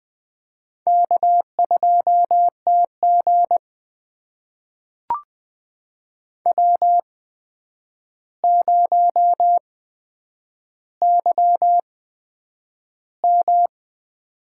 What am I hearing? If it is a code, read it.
Morse code, heard as a keyed tone, about 21 wpm, spaced as for 9 wpm, 700 Hz. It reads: K2TG W0YM
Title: USA Call Signs - 4 Characters - 8x Character Spacing - 20wpm